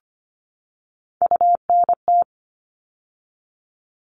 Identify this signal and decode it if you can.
Morse code, heard as a keyed tone, about 25 wpm, 700 Hz. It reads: UNT
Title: All Cut Numbers - 3-Digit 25wpm